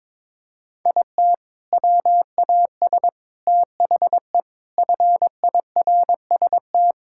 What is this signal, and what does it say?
Morse code, heard as a keyed tone, about 22 wpm, 700 Hz. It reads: IT WAS THE FIRST
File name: practice / 22wpm / Sentences with 4-Word Phrases - 22wpm